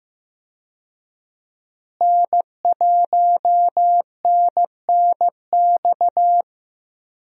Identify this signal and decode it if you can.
Morse code, heard as a keyed tone, about 15 wpm, 700 Hz. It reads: N1NNX
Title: USA Call Signs - 5 Characters - Repeated 3x - 15wpm